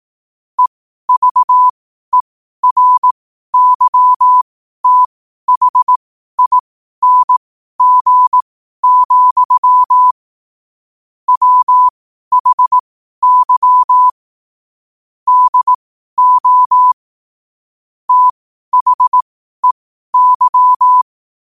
Morse code, heard as a keyed tone, about 18 wpm, spaced as for 12 wpm, 1000 Hz. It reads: EVERYTHING, WHY DO THEY
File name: qotdfiles / 12wpm